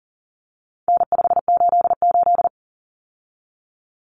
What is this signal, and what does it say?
Morse code, heard as a keyed tone, about 40 wpm, 700 Hz. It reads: N588
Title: Common Cut Numbers - 4-Digit 40wpm